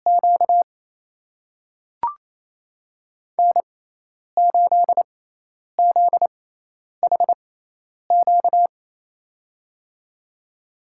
Morse code, heard as a keyed tone, about 27 wpm, spaced as for 10 wpm, 700 Hz. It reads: Q N8ZHQ
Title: USA Call Signs - 5 Characters - 6x Character Spacing - 28wpm